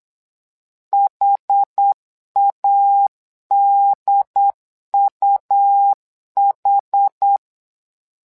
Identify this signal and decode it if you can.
Morse code, heard as a keyed tone, about 8 wpm, 800 Hz. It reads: HADUH
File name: LEZIONI / 2 UHDN / 2 UHD 8wpm 2258